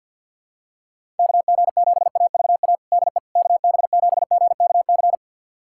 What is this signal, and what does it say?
Morse code, heard as a keyed tone, about 53 wpm, 700 Hz. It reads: KG7AVA DE KB7GKC
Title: USA Callsign DE Callsign - 6 Characters - 50wpm